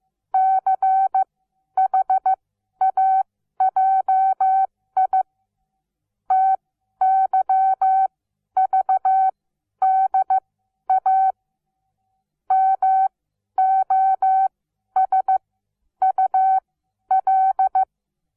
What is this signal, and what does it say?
Morse code, heard as a keyed tone, about 15 wpm, spaced as for 11 wpm, 760 Hz. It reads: CHAJI TYVDA MOSUL
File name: Onlinelehrgang / morsen / DJ4UF Morsen17